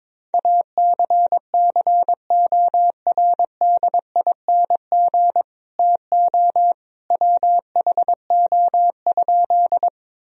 Morse code, heard as a keyed tone, about 21 wpm, 700 Hz. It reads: ACCORDING TO WHO?
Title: Sentences from Top 1000 Words - Review Part 4 - 22wpm